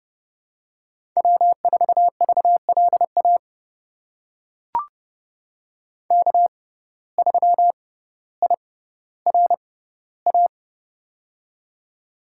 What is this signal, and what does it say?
Morse code, heard as a keyed tone, about 30 wpm, spaced as for 15 wpm, 700 Hz. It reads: W4VLA K3IRA
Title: USA Call Signs - 5 Characters - 6x Character Spacing - 30wpm